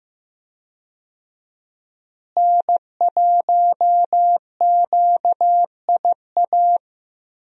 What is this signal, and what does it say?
Morse code, heard as a keyed tone, about 15 wpm, 700 Hz. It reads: N1QIA
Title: USA Call Signs - 5 Characters - Repeated 3x - 15wpm